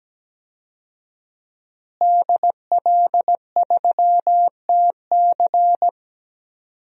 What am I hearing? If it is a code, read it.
Morse code, heard as a keyed tone, about 17 wpm, 700 Hz. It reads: DL3TC
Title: DXCC-100 Call Signs - 5 Characters - Repeated 3x - 17wpm